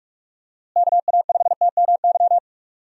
Morse code, heard as a keyed tone, about 46 wpm, 700 Hz. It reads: KA5TMY
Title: USA Call Signs - 6 Characters - Repeated 3x - 45wpm